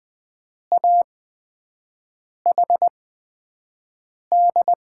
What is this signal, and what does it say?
Morse code, heard as a keyed tone, about 20 wpm, spaced as for 5 wpm, 700 Hz. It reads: AHD
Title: USA Call Signs - 6 Characters - 8x Character Spacing - 20wpm